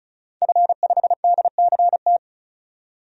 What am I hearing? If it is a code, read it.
Morse code, heard as a keyed tone, about 34 wpm, 700 Hz. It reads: F5DCT
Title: DXCC-100 Call Signs - 5 Characters - 35wpm